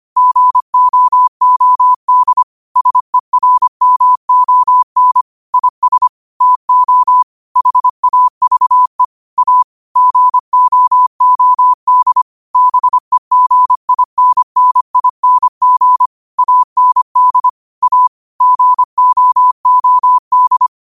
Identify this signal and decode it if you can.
Morse code, heard as a keyed tone, about 25 wpm, 1000 Hz. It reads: GOOD SERMON IS TO HAVE A GOOD BEGINNING AND A GOOD
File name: qotdfiles / 25wpm